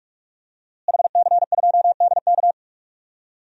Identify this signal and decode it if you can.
Morse code, heard as a keyed tone, about 46 wpm, 700 Hz. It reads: HC2DK
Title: DXCC-100 Call Signs - 5 Characters - Repeated 3x - 45wpm